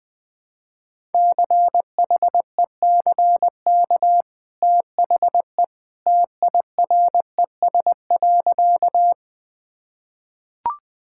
Morse code, heard as a keyed tone, about 21 wpm, 700 Hz. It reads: CHECK THE TIRES.
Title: Easy Sentences with Hints - 20wpm